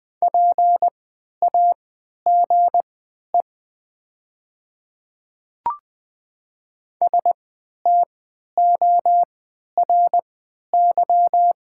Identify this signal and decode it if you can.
Morse code, heard as a keyed tone, about 20 wpm, spaced as for 11 wpm, 700 Hz. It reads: PAGE STORY
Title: 3x Character Spacing - Top 300 Words - 20wpm